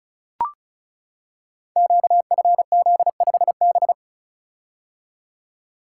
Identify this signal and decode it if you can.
Morse code, heard as a keyed tone, about 36 wpm, 700 Hz. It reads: QFZ5B
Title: Code Groups - 5 Letter:Numbers - 35wpm